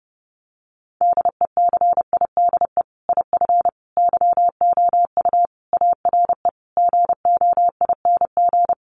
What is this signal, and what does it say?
Morse code, heard as a keyed tone, about 28 wpm, 700 Hz. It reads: DECIDE IF YOU ARE GOING
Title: Sentences from Top 500 Words 30wpm